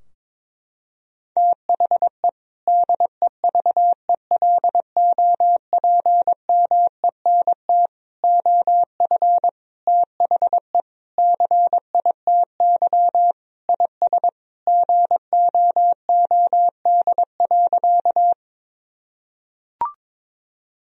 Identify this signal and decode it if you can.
Morse code, heard as a keyed tone, about 22 wpm, 700 Hz. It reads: THE DEVELOPMENT OF THE CITY IS GOOD.
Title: Sentences from Top 1000 Words - Review Part 4 - 22wpm